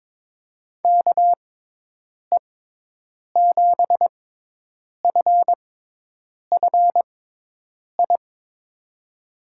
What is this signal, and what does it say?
Morse code, heard as a keyed tone, about 22 wpm, spaced as for 8 wpm, 700 Hz. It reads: KE7FFI
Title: USA Call Signs - 6 Characters - 6x Character Spacing - 22wpm